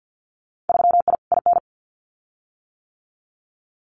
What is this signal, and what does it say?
Morse code, heard as a keyed tone, about 50 wpm, 700 Hz. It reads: 3I IN
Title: Winter Field Day Exchanges 50wpm